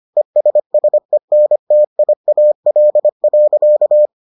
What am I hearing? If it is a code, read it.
Morse code, heard as a keyed tone, about 25 wpm, 600 Hz. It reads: ESSENTIAL.